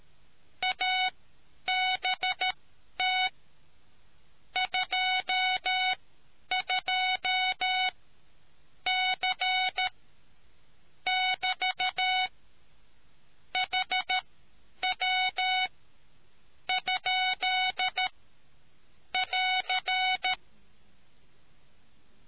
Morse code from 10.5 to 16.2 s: <BT> HW